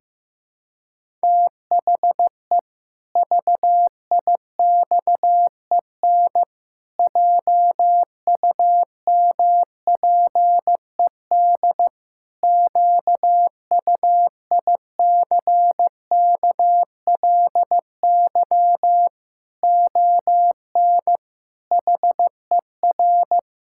Morse code, heard as a keyed tone, about 15 wpm, 700 Hz. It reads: THE VIXEN JUMPED QUICKLY ON HER
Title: Pangram Challenge v2 - 15wpm